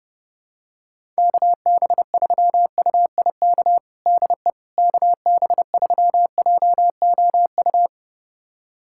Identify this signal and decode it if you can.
Morse code, heard as a keyed tone, about 29 wpm, 700 Hz. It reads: KB3UIK DE KB3JOU
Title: USA Callsign DE Callsign - 6 Characters - 30wpm